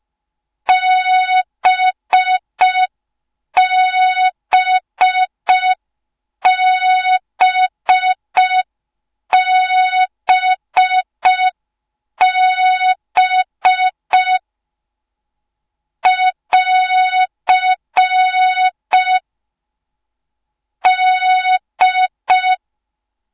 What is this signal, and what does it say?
Morse code, heard as a keyed tone, about 5 wpm, 3700 Hz, with noise. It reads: BBBBB <AR> D
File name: Morze-HA2MN / hang / 06BDU HA2MN